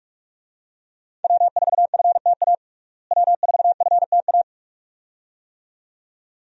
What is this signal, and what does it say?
Morse code, heard as a keyed tone, about 42 wpm, 700 Hz. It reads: W4FTA W4FTA
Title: USA Call Signs - 5 Characters - Repeated 2x - 45wpm